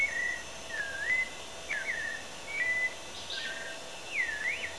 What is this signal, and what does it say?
Emergency Alert System (SAME) tones, heard as harsh buzzing bursts. It is heard from 0.8 to 4.6 s.